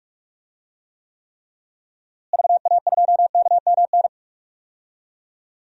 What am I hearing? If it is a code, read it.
Morse code, heard as a keyed tone, about 46 wpm, 700 Hz. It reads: VA2KMN
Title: DXCC-100 Call Signs - 6 Characters - Repeated 3x - 45wpm